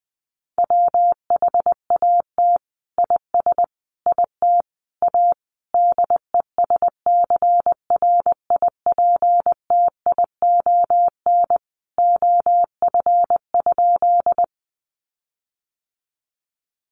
Morse code, heard as a keyed tone, about 20 wpm, 700 Hz. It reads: WHAT IS IT A DESCRIPTION OF?
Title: Sentences from Top 1300 Words 20wpm